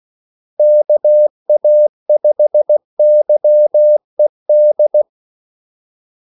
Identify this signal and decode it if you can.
Morse code, heard as a keyed tone, about 16 wpm, 600 Hz. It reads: KA5YED
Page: Kaldesignaler 16 wpm